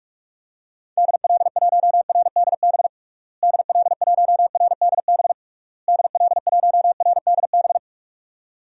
Morse code, heard as a keyed tone, about 47 wpm, 700 Hz. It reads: DL1RDB DL1RDB DL1RDB
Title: DXCC-100 Call Signs - 6 Characters - Repeated 3x - 45wpm